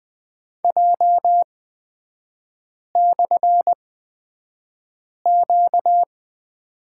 Morse code, heard as a keyed tone, about 20 wpm, spaced as for 5 wpm, 700 Hz. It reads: J/Q
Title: 236 Lesson 38e - Letter q - Three Characters 20wpm